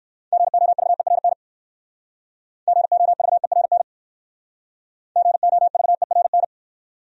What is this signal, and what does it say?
Morse code, heard as a keyed tone, about 57 wpm, 700 Hz. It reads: GOVERN GOVERN GOVERN